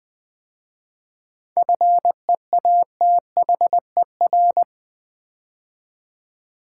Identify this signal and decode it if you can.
Morse code, heard as a keyed tone, about 20 wpm, 700 Hz. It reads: FEATHER
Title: 7 Letter Words - First 3 Letters Given - 20wpm